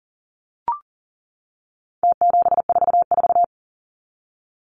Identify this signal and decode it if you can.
Morse code, heard as a keyed tone, about 40 wpm, 700 Hz. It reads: T744
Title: Numbers - Contest Style - 4-Digit 40wpm